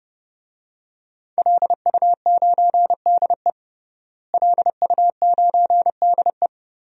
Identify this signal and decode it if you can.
Morse code, heard as a keyed tone, about 32 wpm, 700 Hz. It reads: LU9DE LU9DE
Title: DXCC-100 Call Signs - 5 Characters - Repeated 3x - 30wpm